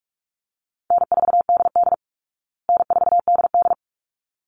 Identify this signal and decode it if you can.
Morse code, heard as a keyed tone, about 44 wpm, 700 Hz. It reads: N4DD N4DD
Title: Call Signs Repeated 3x - 45wpm